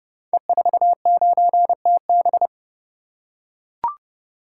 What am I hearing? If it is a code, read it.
Morse code, heard as a keyed tone, about 29 wpm, 700 Hz. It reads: E49TB
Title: Code Groups - 5 Letter:Numbers - 30wpm